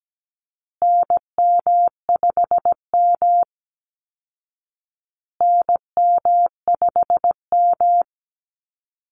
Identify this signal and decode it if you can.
Morse code, heard as a keyed tone, about 17 wpm, 700 Hz. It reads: NM5M NM5M
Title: Call Signs Repeated 2x - 17wpm